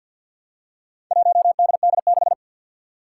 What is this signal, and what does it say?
Morse code, heard as a keyed tone, about 47 wpm, 700 Hz. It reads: 1DD6
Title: Four Challenging Letter-Number Combinations 50wpm